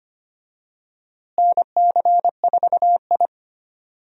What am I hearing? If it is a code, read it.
Morse code, heard as a keyed tone, about 25 wpm, 700 Hz. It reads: NC4I
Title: USA Call Signs - 4 Characters - Repeated 2x - 25wpm